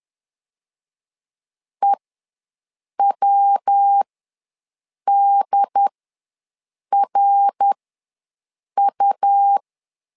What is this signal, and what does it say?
Morse code, heard as a keyed tone, about 10 wpm, spaced as for 6 wpm, 800 Hz. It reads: EWDRU
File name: LEZIONI / 3 MWRO / 3 MWR ws 10wpm 2151